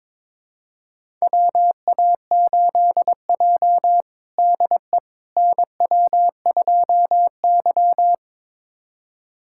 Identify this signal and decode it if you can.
Morse code, heard as a keyed tone, about 22 wpm, 700 Hz. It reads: WA8J DE NW2Y
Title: USA Callsign DE Callsign - 4 Characters - 22wpm